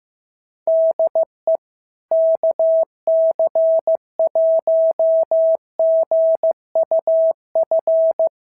Morse code, heard as a keyed tone, about 14 wpm, 650 Hz. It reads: DE KC1GUF
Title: SOTA QSO Practice 15wpm